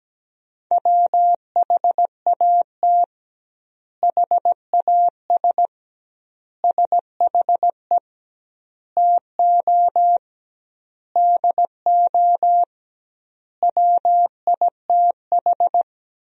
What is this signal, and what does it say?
Morse code, heard as a keyed tone, about 17 wpm, 700 Hz. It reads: WHAT HAS SHE TO DO WITH
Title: 2x Word-Spacing - Sentences from Top 100 Words - 17wpm